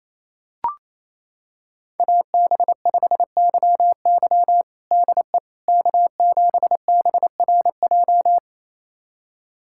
Morse code, heard as a keyed tone, about 28 wpm, 700 Hz. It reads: AB5YY DE K7BRJ
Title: USA Callsign DE Callsign - 5 Characters - 28wpm